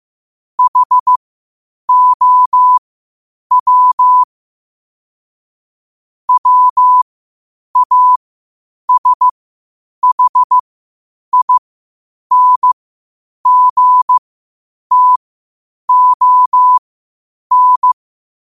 Morse code, heard as a keyed tone, about 15 wpm, spaced as for 8 wpm, 1000 Hz. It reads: HOW WASHINGTON
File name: qotdfiles / 8wpm